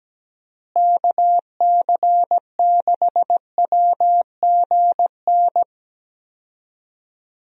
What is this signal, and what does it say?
Morse code, heard as a keyed tone, about 17 wpm, 700 Hz. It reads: KC6WGN
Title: USA Call Signs - 6 Characters - Repeated 2x - 17wpm